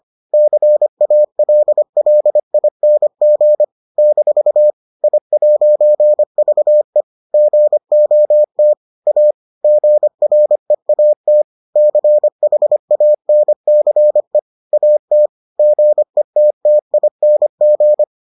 Morse code, heard as a keyed tone, about 25 wpm, 600 Hz. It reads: CALLING - I'VE GOT A GREAT CHANCE AT GETTING